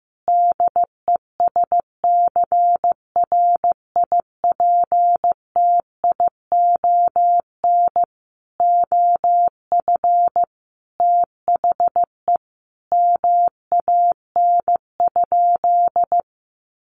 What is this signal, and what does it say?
Morse code, heard as a keyed tone, about 15 wpm, 700 Hz. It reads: DESCRIPTION OF THE MAN?
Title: Sentences from Top 1500 Words 15wpm